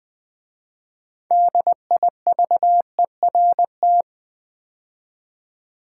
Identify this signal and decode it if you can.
Morse code, heard as a keyed tone, about 20 wpm, 700 Hz. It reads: DIVERT
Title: 6 Letter Words - First 3 Letters Given - 20wpm